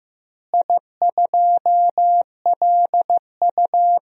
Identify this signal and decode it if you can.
Morse code, heard as a keyed tone, about 15 wpm, 700 Hz. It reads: I2LU
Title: DXCC-100 Call Signs - 4 Characters - Repeated 2x - 15wpm